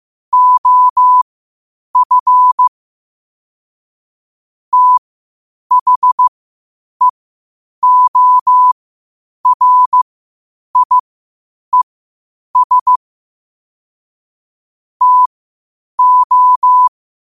Morse code, heard as a keyed tone, about 15 wpm, spaced as for 8 wpm, 1000 Hz. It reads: OF THEORIES TO
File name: qotdfiles / 8wpm